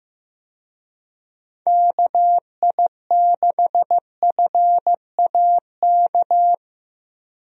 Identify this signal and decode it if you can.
Morse code, heard as a keyed tone, about 15 wpm, 700 Hz. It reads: KI6FAK
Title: USA Call Signs - 6 Characters - Repeated 3x - 15wpm